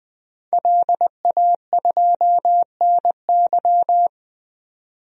Morse code, heard as a keyed tone, about 20 wpm, 700 Hz. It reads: LA2NY
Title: DXCC-100 Call Signs - 5 Characters - 20wpm